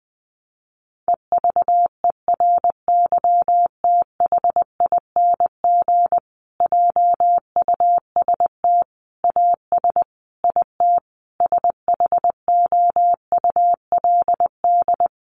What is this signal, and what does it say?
Morse code, heard as a keyed tone, about 20 wpm, 700 Hz. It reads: EVERYTHING JUST AS IT SHOULD
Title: Sentences from Top 700 Words 20wpm